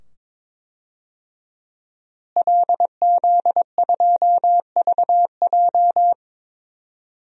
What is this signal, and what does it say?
Morse code, heard as a keyed tone, about 21 wpm, 700 Hz. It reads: LZ2VJ
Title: DXCC-100 Call Signs - 5 Characters - Repeated 2x - 22wpm